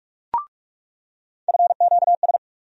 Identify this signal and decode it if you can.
Morse code, heard as a keyed tone, about 45 wpm, 700 Hz. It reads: FQS
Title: USA Call Sign Postfixes - 3 Letters 45wpm